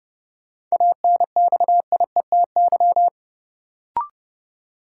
Morse code, heard as a keyed tone, about 30 wpm, 700 Hz. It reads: ANXIETY
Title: Sentences from Top 1600 Words - New Word Drill - 30wpm